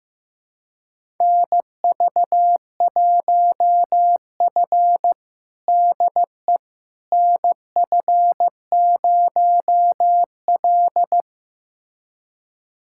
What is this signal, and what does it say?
Morse code, heard as a keyed tone, about 15 wpm, 700 Hz. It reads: NV1F DE NF0L
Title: USA Callsign DE Callsign - 4 Characters - 15wpm